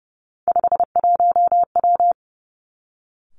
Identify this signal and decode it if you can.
Morse code, heard as a keyed tone, about 31 wpm, 700 Hz. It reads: 51W
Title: Three Challenging Letter-Number Combinations 30wpm